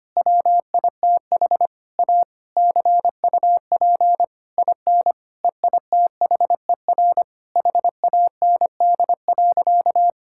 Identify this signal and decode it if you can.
Morse code, heard as a keyed tone, about 25 wpm, 700 Hz. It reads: WITH A CUP IN EITHER HAND.
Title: Sentences from Top 1600 Words 25wpm